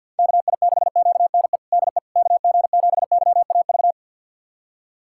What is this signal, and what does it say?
Morse code, heard as a keyed tone, about 50 wpm, 700 Hz. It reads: KI6QNE DE KG7YAV
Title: USA Callsign DE Callsign 50wpm